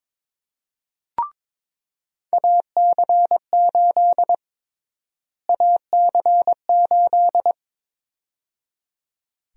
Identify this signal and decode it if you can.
Morse code, heard as a keyed tone, about 23 wpm, 700 Hz. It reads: AC8 AC8
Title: USA Call Sign Prefixes - Repeated 2x - 22wpm